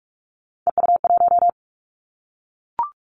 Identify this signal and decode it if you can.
Morse code, heard as a keyed tone, about 46 wpm, 700 Hz. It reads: EU1
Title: Callsign Prefixes for Top 50 DXCC - 45wpm